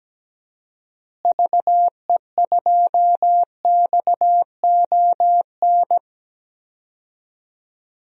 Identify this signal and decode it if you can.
Morse code, heard as a keyed tone, about 17 wpm, 700 Hz. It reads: VE2XON
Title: DXCC-100 Call Signs - 6 Characters - Repeated 2x - 17wpm